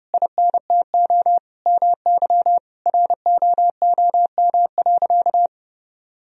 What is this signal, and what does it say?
Morse code, heard as a keyed tone, about 30 wpm, 700 Hz. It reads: INTO MY ROOM.
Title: Sentences from Top 1000 Words - Review Part 1 - 30wpm